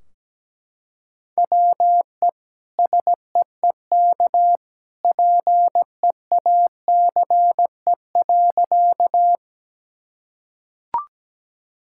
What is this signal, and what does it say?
Morse code, heard as a keyed tone, about 17 wpm, 700 Hz. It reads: WE SEEK PEACE.
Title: Sentences from Top 1600 Words 17wpm